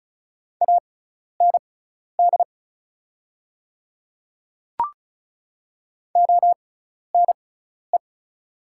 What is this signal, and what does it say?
Morse code, heard as a keyed tone, about 36 wpm, spaced as for 12 wpm, 700 Hz. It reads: AND ONE
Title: 6x Character Spacing - Top 300 Words - 35wpm